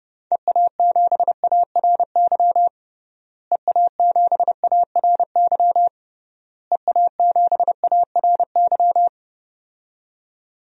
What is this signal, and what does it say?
Morse code, heard as a keyed tone, about 32 wpm, 700 Hz. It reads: EA7ARY EA7ARY EA7ARY
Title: DXCC-100 Call Signs - 6 Characters - Repeated 3x - 30wpm